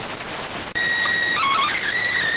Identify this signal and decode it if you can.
SSTV (slow-scan television), PD 290.
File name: findings / 81 axial iodine 262.gen-34 rx